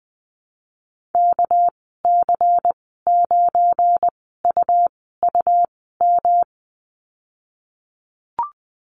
Morse code, heard as a keyed tone, about 20 wpm, spaced as for 15 wpm, 700 Hz. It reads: KC9UUM
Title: 238 Lesson 38g - Letter q - Call Signs 20wpm